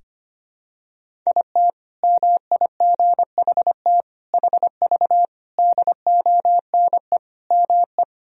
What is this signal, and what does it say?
Morse code, heard as a keyed tone, about 25 wpm, 700 Hz. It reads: IT MIGHT HV DONE ME
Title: Sentences from Top 500 Words and Abbreviations 25wpm